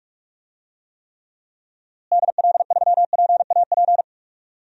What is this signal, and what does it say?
Morse code, heard as a keyed tone, about 46 wpm, 700 Hz. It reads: DL3PAP
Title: DXCC-100 Call Signs - 6 Characters - Repeated 2x - 45wpm